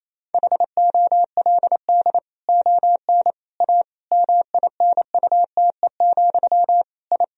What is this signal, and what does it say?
Morse code, heard as a keyed tone, about 28 wpm, 700 Hz. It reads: HOLD ON A MINUTE, I
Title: Sentences from Top 1000 Words - Review Part 2 - 28wpm